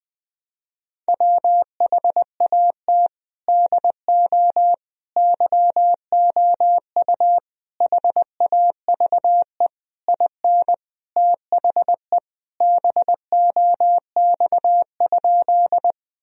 Morse code, heard as a keyed tone, about 20 wpm, 700 Hz. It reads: WHAT DO YOU HAVE IN THE BOX?
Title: Sentences from Top 1000 Words - Review Part 2 - 20wpm